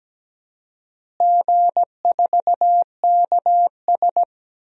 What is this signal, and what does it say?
Morse code, heard as a keyed tone, about 17 wpm, 700 Hz. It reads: G4KS
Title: DXCC-100 Call Signs - 4 Characters - Repeated 2x - 17wpm